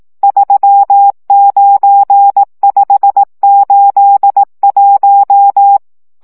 Morse code, heard as a keyed tone, about 18 wpm, 800 Hz. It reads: 39581
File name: morze vizsga